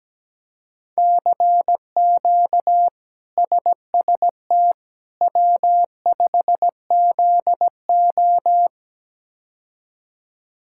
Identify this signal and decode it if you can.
Morse code, heard as a keyed tone, about 17 wpm, 700 Hz. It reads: CQ SST W5ZO
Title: K1USN SST QSO Practice - 17wpm